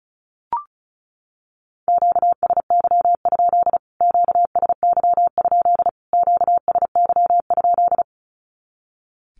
Morse code, heard as a keyed tone, about 34 wpm, 700 Hz. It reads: QSY? QSY? QSY?
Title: Q-Codes Repeated 3x 35wpm